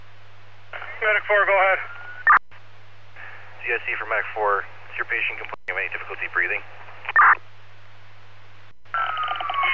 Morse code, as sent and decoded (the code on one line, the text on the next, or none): none